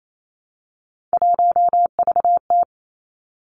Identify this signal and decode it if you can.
Morse code, heard as a keyed tone, about 28 wpm, 700 Hz. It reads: 1VT
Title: Three Challenging Letter-Number Combinations 28wpm